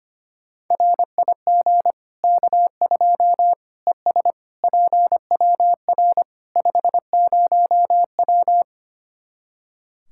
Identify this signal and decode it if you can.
Morse code, heard as a keyed tone, about 25 wpm, 700 Hz. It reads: RIG K2 ES PWR 50W
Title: Practice QSO - Exchange 2 - Info - 25wpm